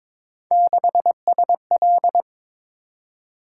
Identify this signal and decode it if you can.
Morse code, heard as a keyed tone, about 22 wpm, 700 Hz. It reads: 6SL
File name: practice / 22wpm / Three Letter-Number Combinations - 22wpm